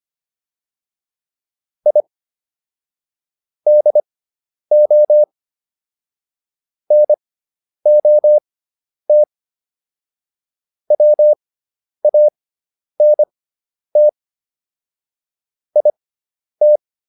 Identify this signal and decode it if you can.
Morse code, heard as a keyed tone, about 25 wpm, spaced as for 10 wpm, 600 Hz. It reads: I DO NOT WANT IT